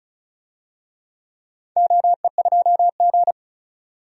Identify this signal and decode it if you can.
Morse code, heard as a keyed tone, about 34 wpm, 700 Hz. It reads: OE2G
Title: DXCC-100 Call Signs - 4 Characters - 35wpm